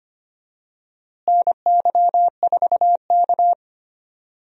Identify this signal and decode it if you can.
Morse code, heard as a keyed tone, about 25 wpm, 700 Hz. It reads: NY4K
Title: USA Call Signs - 4 Characters - Repeated 3x - 25wpm